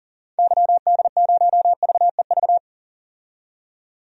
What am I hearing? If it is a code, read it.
Morse code, heard as a keyed tone, about 40 wpm, 700 Hz. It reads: YD0VEV
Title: DXCC-100 Call Signs - 6 Characters - 40wpm